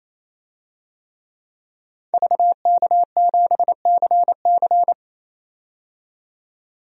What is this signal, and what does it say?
Morse code, heard as a keyed tone, about 28 wpm, 700 Hz. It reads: VK7CC
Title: DXCC-100 Call Signs - 5 Characters - 28wpm